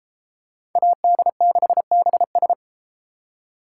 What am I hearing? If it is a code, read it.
Morse code, heard as a keyed tone, about 30 wpm, 700 Hz. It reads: AD6BS